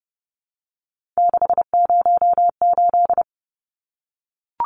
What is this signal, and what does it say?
Morse code, heard as a keyed tone, about 30 wpm, 700 Hz. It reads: 608
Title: Numbers - 3-Digit 30wpm